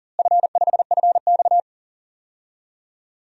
Morse code, heard as a keyed tone, about 40 wpm, 700 Hz. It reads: F5FX
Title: DXCC-100 Call Signs - 4 Characters - Repeated 3x - 40wpm